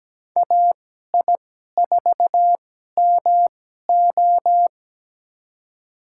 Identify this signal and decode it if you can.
Morse code, heard as a keyed tone, about 17 wpm, spaced as for 12 wpm, 700 Hz. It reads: AI4MO
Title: USA Call Signs - 5 Characters - 2x Character Spacing - 17wpm